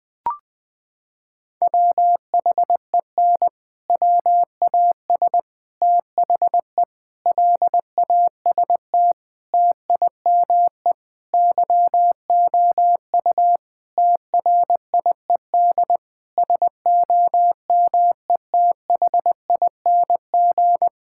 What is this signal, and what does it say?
Morse code, heard as a keyed tone, about 21 wpm, 700 Hz. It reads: WHEN WAS THE LAST TIME YOU TRIED SOMETHING
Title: Sentences with 6-Word Phrases - 20wpm